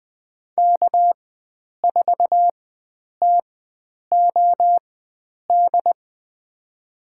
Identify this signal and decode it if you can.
Morse code, heard as a keyed tone, about 20 wpm, spaced as for 9 wpm, 700 Hz. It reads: K4TOD
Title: USA Call Signs - 5 Characters - 4x Character Spacing - 20wpm